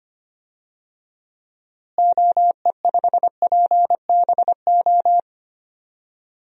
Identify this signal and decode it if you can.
Morse code, heard as a keyed tone, about 25 wpm, 700 Hz. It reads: OE5PBO
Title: DXCC-100 Call Signs - 6 Characters - Repeated 2x - 25wpm